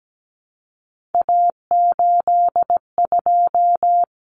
Morse code, heard as a keyed tone, about 17 wpm, 700 Hz. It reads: A82
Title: Common Cut Numbers - 3-Digit 17wpm